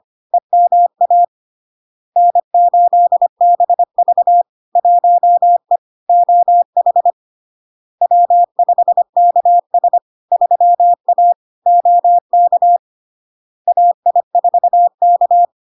Morse code, heard as a keyed tone, about 25 wpm, 700 Hz. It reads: EMA N8BV 1E OH W5KS 3A OK AI4K
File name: Log-in-CW-25wpm-700-Hz